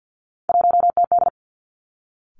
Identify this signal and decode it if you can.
Morse code, heard as a keyed tone, about 50 wpm, 700 Hz. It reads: 1TD
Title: Three Challenging Letter-Number Combinations 50wpm